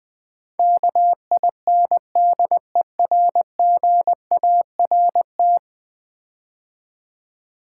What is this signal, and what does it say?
Morse code, heard as a keyed tone, about 20 wpm, 700 Hz. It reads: KINDERGART
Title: Build A Word - 5 Letters At A Time - 12-Letter Words - 20wpm